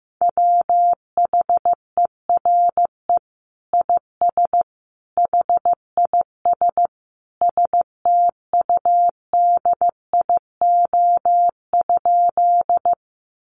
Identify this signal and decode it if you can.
Morse code, heard as a keyed tone, about 15 wpm, 700 Hz. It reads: WHERE IS HIS STUDIO?
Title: Sentences from Top 1200 Words 15wpm